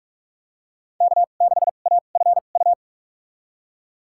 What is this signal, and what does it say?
Morse code, heard as a keyed tone, about 46 wpm, spaced as for 33 wpm, 700 Hz. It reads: K6AFU
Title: USA Call Signs - 2x Character Spacing - 45wpm